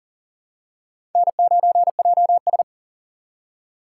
Morse code, heard as a keyed tone, about 40 wpm, 700 Hz. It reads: N9JS
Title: USA Call Signs - 4 Characters - 6x Character Spacing - 40wpm